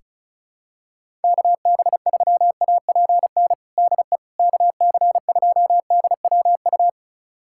Morse code, heard as a keyed tone, about 35 wpm, 700 Hz. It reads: KB3APN DE KC2DWU
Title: USA Callsign DE Callsign - 6 Characters - 35wpm